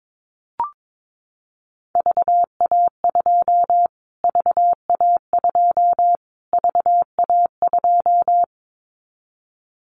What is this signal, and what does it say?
Morse code, heard as a keyed tone, about 22 wpm, 700 Hz. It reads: VA2 VA2 VA2
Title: Callsign Prefixes for Top 25 DXCC - Repeated 3x - 22wpm